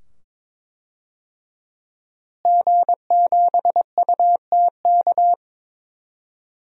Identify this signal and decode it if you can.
Morse code, heard as a keyed tone, about 23 wpm, 700 Hz. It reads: G7UTK
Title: DXCC-100 Call Signs - 5 Characters - Repeated 2x - 22wpm